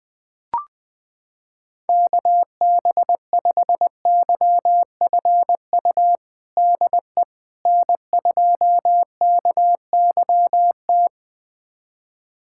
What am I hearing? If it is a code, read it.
Morse code, heard as a keyed tone, about 20 wpm, 700 Hz. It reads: KB5YFU DE N2KYT
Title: USA Callsign DE Callsign 20wpm